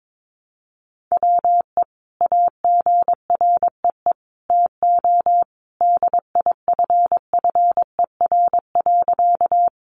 Morse code, heard as a keyed tone, about 22 wpm, 700 Hz. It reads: WE AGREE TO DIFFER.
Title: Sentences from Top 1100 Words 22wpm